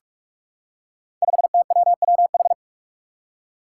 Morse code, heard as a keyed tone, about 44 wpm, 700 Hz. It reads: 5TWWH
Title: Five Challenging Letter-Number Combinations 45wpm